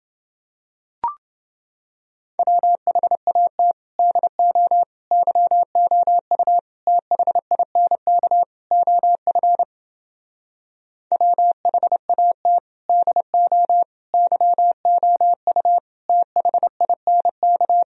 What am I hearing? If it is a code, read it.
Morse code, heard as a keyed tone, about 31 wpm, 700 Hz. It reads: WHAT DO YOU THINK OF WHAT DO YOU THINK